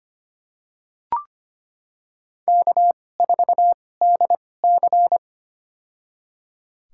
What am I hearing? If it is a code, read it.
Morse code, heard as a keyed tone, about 25 wpm, spaced as for 18 wpm, 700 Hz. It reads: K4DC
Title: USA Call Signs - 2x Character Spacing - 25wpm